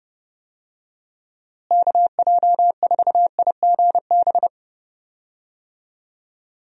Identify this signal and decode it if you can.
Morse code, heard as a keyed tone, about 30 wpm, 700 Hz. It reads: KJ4IGB
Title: USA Call Signs - 6 Characters - Repeated 3x - 30wpm